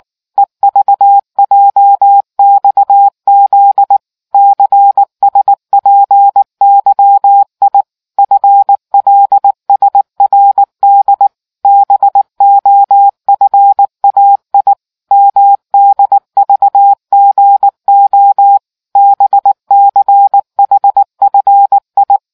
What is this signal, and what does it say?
Morse code, heard as a keyed tone, about 19 wpm, 800 Hz. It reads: EVJXZ CSPYI FLSRD BOFAI MDVGO BCHFI